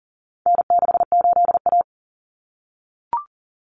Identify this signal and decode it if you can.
Morse code, heard as a keyed tone, about 40 wpm, 700 Hz. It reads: N68A
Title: Common Cut Numbers - 4-Digit 40wpm